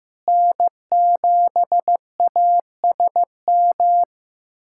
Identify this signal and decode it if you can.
Morse code, heard as a keyed tone, about 15 wpm, 700 Hz. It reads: N7ASM
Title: USA Callsign DE Callsign - 5 Characters - 15wpm